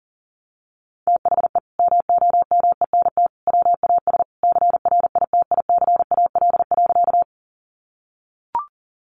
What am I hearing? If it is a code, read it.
Morse code, heard as a keyed tone, about 40 wpm, 700 Hz. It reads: THE MOMENT WAS CRITICAL.
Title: Sentences from Top 1100 Words 40wpm